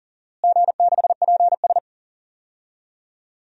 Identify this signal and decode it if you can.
Morse code, heard as a keyed tone, about 40 wpm, 700 Hz. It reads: G6PS
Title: DXCC-100 Call Signs - 4 Characters - Repeated 2x - 40wpm